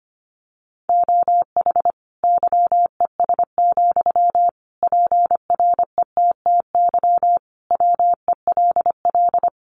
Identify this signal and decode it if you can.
Morse code, heard as a keyed tone, about 25 wpm, 700 Hz. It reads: OH YES, PRETTY WELL
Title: Sentences from Top 1000 Words 25wpm